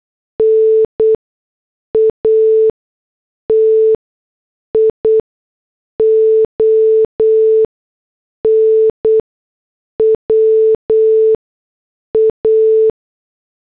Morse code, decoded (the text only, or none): NATIONWA